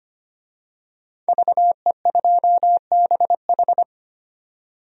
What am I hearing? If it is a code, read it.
Morse code, heard as a keyed tone, about 25 wpm, 700 Hz. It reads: VE2BH
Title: DXCC-100 Call Signs - 5 Characters - 25wpm